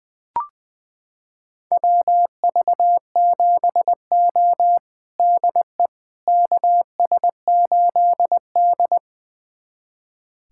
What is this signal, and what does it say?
Morse code, heard as a keyed tone, about 20 wpm, 700 Hz. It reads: WV7O DE KS8D
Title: USA Callsign DE Callsign - 4 Characters - 20wpm